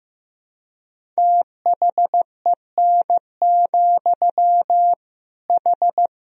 Morse code, heard as a keyed tone, about 16 wpm, 700 Hz. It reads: THEN, H
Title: Sentences from Top 100 Words and Abbreviations 15wpm